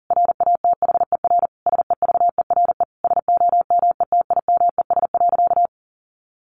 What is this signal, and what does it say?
Morse code, heard as a keyed tone, about 40 wpm, 700 Hz. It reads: RATHER SEVERE SOMETIMES.
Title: Sentences from Top 1400 Words 40wpm